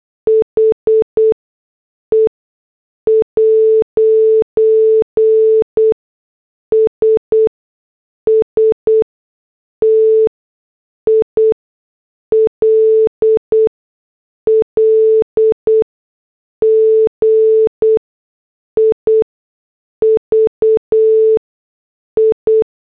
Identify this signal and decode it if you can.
Morse code, heard as a keyed tone, about 8 wpm, spaced as for 6 wpm, 440 Hz. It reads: HE'S STILL GIVI
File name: morse / headlines